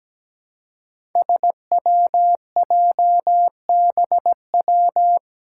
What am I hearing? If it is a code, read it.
Morse code, heard as a keyed tone, about 17 wpm, 700 Hz. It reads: SWJBW
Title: Five Challenging Letter-Number Combinations 17wpm